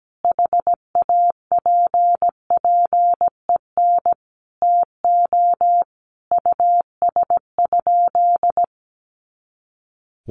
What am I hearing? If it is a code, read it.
Morse code, heard as a keyed tone, about 17 wpm, 700 Hz. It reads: HAPPEN TO US?
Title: Sentences from Top 400 Words 17wpm